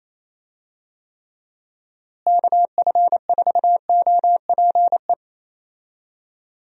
Morse code, heard as a keyed tone, about 28 wpm, 700 Hz. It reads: KF4OPE
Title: USA Call Signs - 6 Characters - Repeated 2x - 28wpm